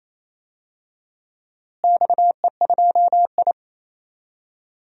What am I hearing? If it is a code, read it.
Morse code, heard as a keyed tone, about 28 wpm, 700 Hz. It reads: XE2I
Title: DXCC-100 Call Signs - 4 Characters - Repeated 2x - 28wpm